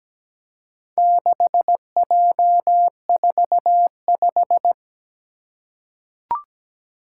Morse code, heard as a keyed tone, about 17 wpm, 700 Hz. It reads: 6J45
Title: Four Challenging Letter-Number Combinations 17wpm